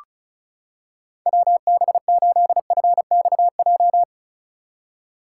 Morse code, heard as a keyed tone, about 36 wpm, 700 Hz. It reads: WB8FXJ
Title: K1USN SST QSO Practice - 35wpm